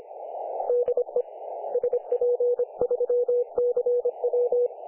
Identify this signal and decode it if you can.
Morse code, heard as a keyed tone, about 25 wpm, 500 Hz, with radio noise. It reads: DE SP3CW
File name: dlazaawansowanych / nagraniazpasma / websdr recording 3564.7kHz